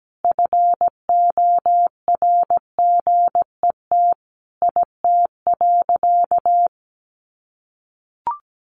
Morse code, heard as a keyed tone, about 17 wpm, 700 Hz. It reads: FORGET IT.
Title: Sentences from Top 1200 Words 17wpm